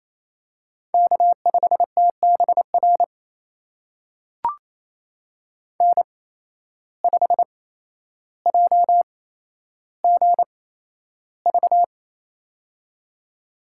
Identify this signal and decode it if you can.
Morse code, heard as a keyed tone, about 27 wpm, spaced as for 12 wpm, 700 Hz. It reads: K5TBR N5JGV
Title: USA Call Signs - 8x Character Spacing - 28wpm